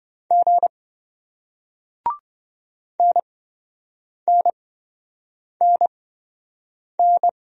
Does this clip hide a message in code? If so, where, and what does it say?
Yes: Morse code from 0.3 to 7.3 s, heard as a keyed tone, about 27 wpm, spaced as for 7 wpm, 700 Hz. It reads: G NNNN